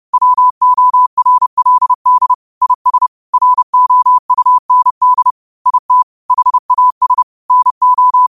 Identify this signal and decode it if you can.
Morse code, heard as a keyed tone, about 30 wpm, 1000 Hz. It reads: WORLD IS ROUND IT HAS NO
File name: qotdfiles / 30wpm